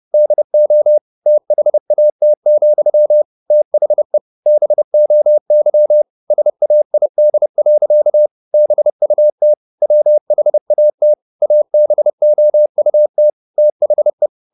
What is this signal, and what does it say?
Morse code, heard as a keyed tone, about 30 wpm, 600 Hz. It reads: DO THAT, THE BOY SAID. BUT WHAT ABOUT THE
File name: the old man and the sea / the old man and the sea 1 30wpm